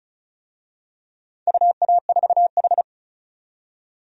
Morse code, heard as a keyed tone, about 34 wpm, 700 Hz. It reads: UA4H
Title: DXCC-100 Call Signs - 4 Characters - 35wpm